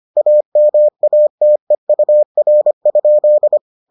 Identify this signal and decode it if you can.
Morse code, heard as a keyed tone, about 25 wpm, 600 Hz. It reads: AMATEUR?